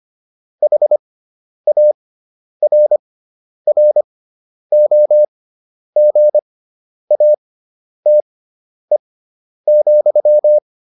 Morse code, heard as a keyed tone, about 25 wpm, spaced as for 10 wpm, 600 Hz. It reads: HARROGATE,
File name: QSO404 10